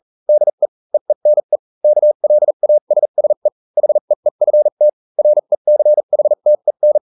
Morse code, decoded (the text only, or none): DE EENE KLASSE HEEFT RECHTEN